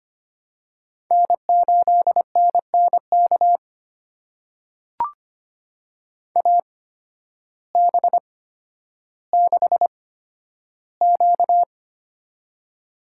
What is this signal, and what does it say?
Morse code, heard as a keyed tone, about 25 wpm, spaced as for 12 wpm, 700 Hz. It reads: N8NNK AB6Q
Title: USA Call Signs - 8x Character Spacing - 25wpm